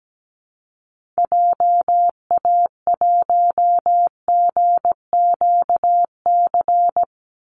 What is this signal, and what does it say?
Morse code, heard as a keyed tone, about 17 wpm, 700 Hz. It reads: JA1GQC
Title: Call Signs - 17wpm